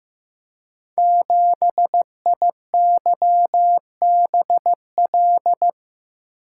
Morse code, heard as a keyed tone, about 15 wpm, 700 Hz. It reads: 7IYBL
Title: Code Groups - 5 Letter:Numbers - 15wpm